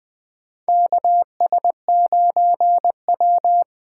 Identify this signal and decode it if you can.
Morse code, heard as a keyed tone, about 20 wpm, 700 Hz. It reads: KS9W
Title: K1USN SST QSO Practice - 20wpm